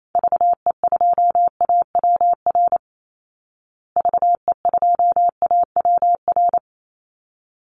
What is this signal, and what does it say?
Morse code, heard as a keyed tone, about 28 wpm, 700 Hz. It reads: VE2AWR VE2AWR
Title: Call Signs Repeated 3x - 28wpm